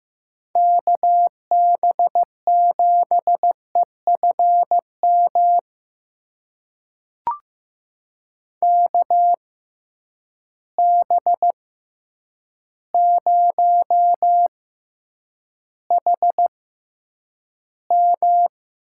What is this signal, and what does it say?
Morse code, heard as a keyed tone, about 15 wpm, spaced as for 8 wpm, 700 Hz. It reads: KB7EFM KB0HM
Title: USA Call Signs - 6 Characters - 6x Character Spacing - 15wpm